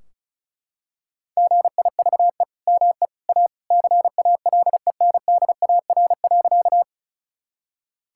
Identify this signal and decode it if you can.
Morse code, heard as a keyed tone, about 34 wpm, 700 Hz. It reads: GIVE ME A CALENDAR.
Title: Sentences from Top 1600 Words 35wpm